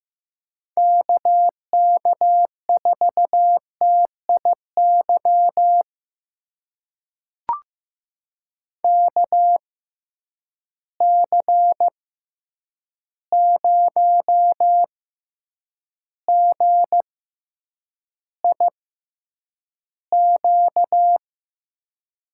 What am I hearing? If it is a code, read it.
Morse code, heard as a keyed tone, about 15 wpm, spaced as for 8 wpm, 700 Hz. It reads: KK4TIY KC0GIQ
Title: USA Call Signs - 6 Characters - 6x Character Spacing - 15wpm